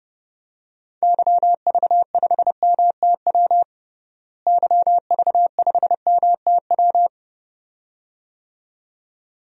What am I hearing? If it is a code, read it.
Morse code, heard as a keyed tone, about 30 wpm, 700 Hz. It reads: YV5MTW YV5MTW
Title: DXCC-100 Call Signs - 6 Characters - Repeated 2x - 30wpm